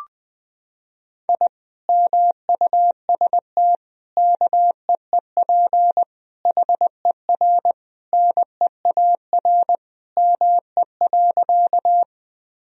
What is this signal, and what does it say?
Morse code, heard as a keyed tone, about 20 wpm, 700 Hz. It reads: I MUST KEEP HER NEAR ME.
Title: Sentences from Top 1000 Words - Review Part 1 - 20wpm